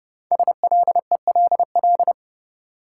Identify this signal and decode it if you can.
Morse code, heard as a keyed tone, about 30 wpm, 700 Hz. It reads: SLELL
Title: Code Groups - 5 Letters - 30wpm